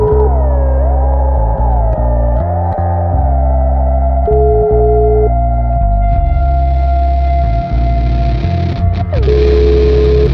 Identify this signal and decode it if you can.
Ringback tone (Europe, 425 Hz), with noise.